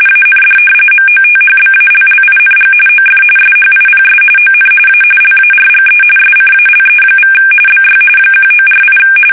RTTY, about 45.45 baud. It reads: AND MOTHBALLING OF SHIPS ON A LARGE SCALE= BY 1948
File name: fsk850 / usnavy5